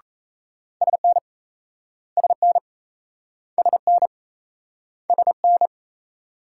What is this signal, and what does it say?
Morse code, heard as a keyed tone, about 36 wpm, 700 Hz. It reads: SN SN SN SN